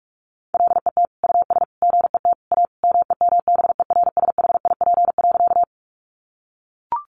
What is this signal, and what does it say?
Morse code, heard as a keyed tone, about 44 wpm, 700 Hz. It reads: LET US GET A MEMBERSHIP.
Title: Sentences from Top 1400 Words 45wpm